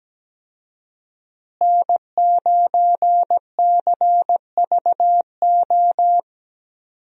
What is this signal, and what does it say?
Morse code, heard as a keyed tone, about 17 wpm, 700 Hz. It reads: N9CVO
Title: USA Call Signs - 5 Characters - Repeated 3x - 17wpm